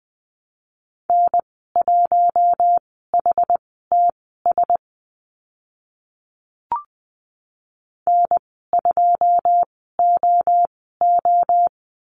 Morse code, heard as a keyed tone, about 20 wpm, spaced as for 14 wpm, 700 Hz. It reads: N1HTS N2OO
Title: 088 Lesson 15g - Number 5 - Call Signs 20wpm